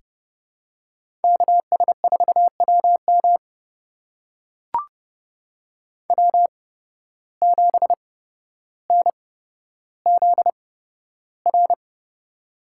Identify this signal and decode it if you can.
Morse code, heard as a keyed tone, about 30 wpm, spaced as for 13 wpm, 700 Hz. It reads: KS4WM W7NZR
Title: USA Call Signs - 8x Character Spacing - 30wpm